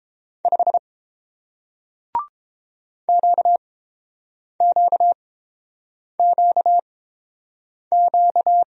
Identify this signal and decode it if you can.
Morse code, heard as a keyed tone, about 30 wpm, spaced as for 7 wpm, 700 Hz. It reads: 5 QQQQ